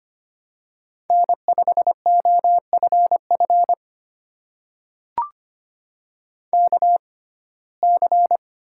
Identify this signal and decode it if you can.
Morse code, heard as a keyed tone, about 25 wpm, spaced as for 18 wpm, 700 Hz. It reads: N5OFF KC
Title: USA Call Signs - 6x Character Spacing - 25wpm